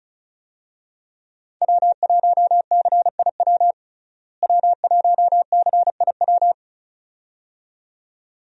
Morse code, heard as a keyed tone, about 34 wpm, 700 Hz. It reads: W1CIW W1CIW
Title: USA Call Signs - 5 Characters - Repeated 2x - 35wpm